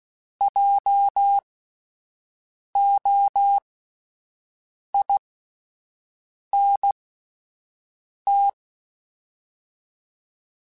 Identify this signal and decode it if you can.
Morse code, heard as a keyed tone, about 16 wpm, spaced as for 5 wpm, 800 Hz. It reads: JOINT
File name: Spaced Alphas / Spaced Alphas mp3s / Js sp @10